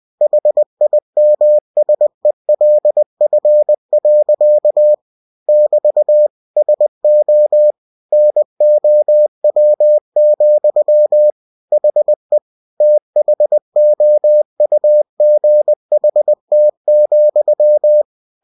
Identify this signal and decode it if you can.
Morse code, heard as a keyed tone, about 20 wpm, 600 Hz. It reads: HIMSELF. <BT> SO NOW, HE THOUGHT,
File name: cw-books / winnie-the-pooh / winnie-20wpm-page-0093